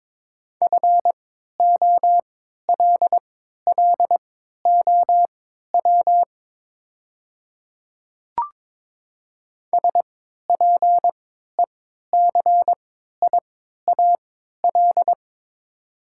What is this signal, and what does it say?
Morse code, heard as a keyed tone, about 22 wpm, spaced as for 13 wpm, 700 Hz. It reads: FOLLOW SPECIAL
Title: 3x Character Spacing - Top 500 Words - 22wpm